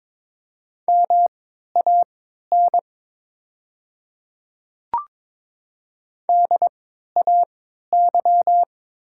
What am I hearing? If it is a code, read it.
Morse code, heard as a keyed tone, about 22 wpm, spaced as for 13 wpm, 700 Hz. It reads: MAN DAY
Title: 3x Character Spacing - Top 200 Words - 22wpm